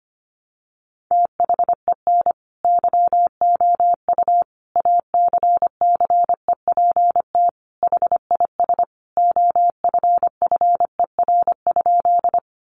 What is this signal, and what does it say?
Morse code, heard as a keyed tone, about 25 wpm, 700 Hz. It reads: THEN YOU ACCEPT HIS OFFER?
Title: Sentences from Top 1300 Words 25wpm